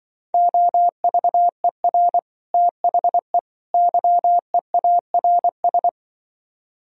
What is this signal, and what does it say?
Morse code, heard as a keyed tone, about 24 wpm, 700 Hz. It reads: OVER THE YEARS
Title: Speed-Racing - Sets of 3 Words 20wpm